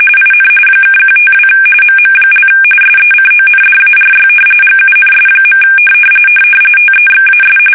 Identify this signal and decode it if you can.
RTTY, about 45.45 baud. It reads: IN (152 MM) SECONDARY GUNS. TEXAS
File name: fsk850 / battleships